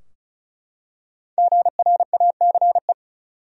Morse code, heard as a keyed tone, about 36 wpm, 700 Hz. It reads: GRACE
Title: USA Common Names - Max 5 Letters - 35wpm